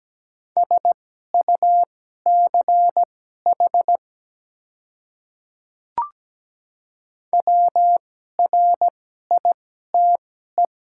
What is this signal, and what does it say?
Morse code, heard as a keyed tone, about 17 wpm, spaced as for 12 wpm, 700 Hz. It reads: SUCH WRITE
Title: 2x Character Spacing - Top 200 Words - 17wpm